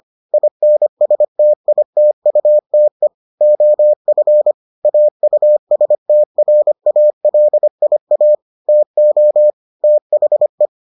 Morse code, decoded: INSTITUTE OF AUSTRALIA TO THE

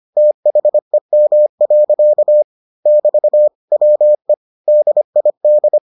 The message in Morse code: THEM. <BT> WE DID

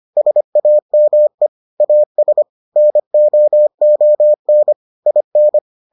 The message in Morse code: SAME AS NOON IN

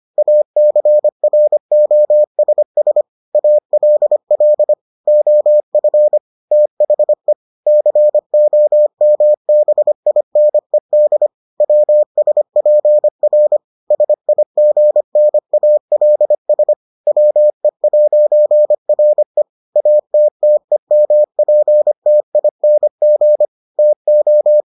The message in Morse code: ACROSS ALL OF THE COMBINED WSPR SIGNALS WE'RE ATTEMPTING TO

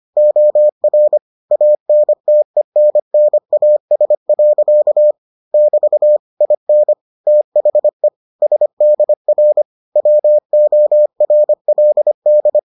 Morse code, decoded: OR ANTENNAS. <BT> IN THE SDR WORLD

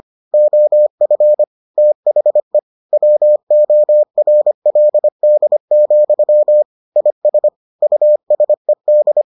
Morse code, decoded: OF THE WORLD, IS USED

